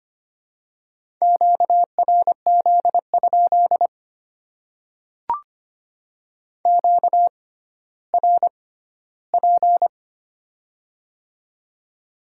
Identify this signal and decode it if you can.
Morse code, heard as a keyed tone, about 25 wpm, spaced as for 14 wpm, 700 Hz. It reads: QRZ? QRP
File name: Top 90 QSO Elements - 6x Character Spacing - 25wpm